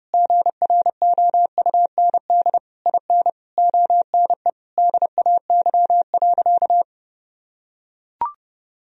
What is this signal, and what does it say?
Morse code, heard as a keyed tone, about 30 wpm, 700 Hz. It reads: GROUND IN ONE DAY.
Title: Sentences from Top 1000 Words - Review Part 3 - 30wpm